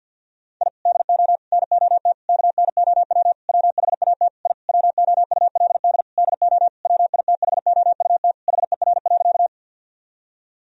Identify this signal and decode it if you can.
Morse code, heard as a keyed tone, about 50 wpm, 700 Hz. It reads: I DO NOT KNOW WHAT I WOULD DO WITHOUT HER.
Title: Sentences from Top 1000 Words - Review Part 3 - 50wpm